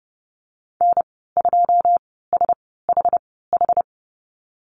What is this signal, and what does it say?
Morse code, heard as a keyed tone, about 30 wpm, spaced as for 17 wpm, 700 Hz. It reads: N2SHH
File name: Lesson 15g - Number 5 - Call Signs 30wpm